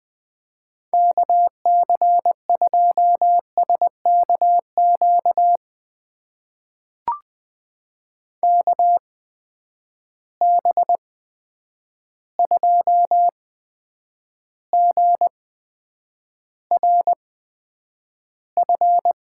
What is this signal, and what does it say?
Morse code, heard as a keyed tone, about 20 wpm, spaced as for 9 wpm, 700 Hz. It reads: KC2SKQ KB2GRF
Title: USA Call Signs - 6 Characters - 8x Character Spacing - 20wpm